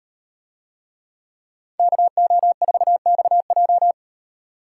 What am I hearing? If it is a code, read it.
Morse code, heard as a keyed tone, about 38 wpm, 700 Hz. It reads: KO4XJ